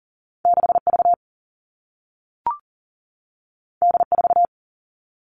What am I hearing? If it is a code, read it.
Morse code, heard as a keyed tone, about 41 wpm, 700 Hz. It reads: 6V D4
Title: All Cut Numbers - 2-Digit 40wpm